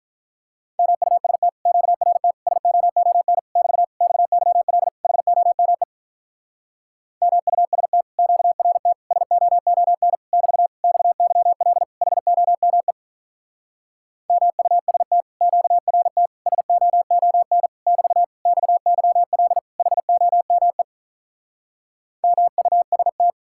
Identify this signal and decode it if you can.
Morse code, heard as a keyed tone, about 46 wpm, 700 Hz. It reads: MUST QRT SOON <BT> XYL HOME MUST QRT SOON <BT> XYL HOME MUST QRT SOON <BT> XYL HOME MUST